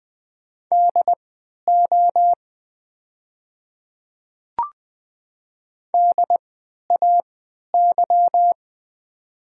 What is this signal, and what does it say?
Morse code, heard as a keyed tone, about 20 wpm, spaced as for 11 wpm, 700 Hz. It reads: DO DAY
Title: 3x Character Spacing - Top 500 Words - 20wpm